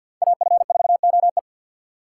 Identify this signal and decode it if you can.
Morse code, heard as a keyed tone, about 53 wpm, 700 Hz. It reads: AF4OE
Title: USA Call Signs - 5 Characters - 2x Character Spacing - 50wpm